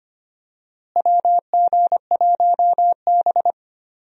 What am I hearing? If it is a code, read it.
Morse code, heard as a keyed tone, about 25 wpm, 700 Hz. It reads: WG1B
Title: USA Call Signs - 4 Characters - Repeated 2x - 25wpm